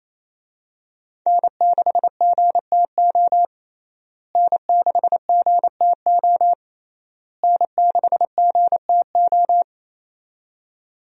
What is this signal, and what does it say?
Morse code, heard as a keyed tone, about 28 wpm, 700 Hz. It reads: N6GTO N6GTO N6GTO
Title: USA Call Signs - 5 Characters - Repeated 3x - 28wpm